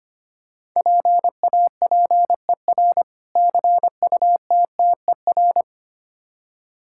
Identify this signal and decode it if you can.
Morse code, heard as a keyed tone, about 25 wpm, 700 Hz. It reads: PAPER CUTTER
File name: practice / 25wpm / Difficult Words with Hints - 25wpm